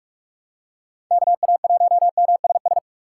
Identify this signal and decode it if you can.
Morse code, heard as a keyed tone, about 44 wpm, 700 Hz. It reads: KA1MSS
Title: USA Call Signs - 6x Character Spacing - 45wpm